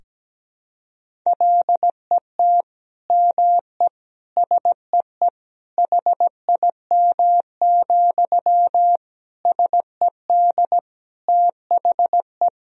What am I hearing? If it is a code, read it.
Morse code, heard as a keyed tone, about 17 wpm, 700 Hz. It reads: LET ME SEE HIM, SED THE
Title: Sentences from Top 400 Words and Abbreviations 17wpm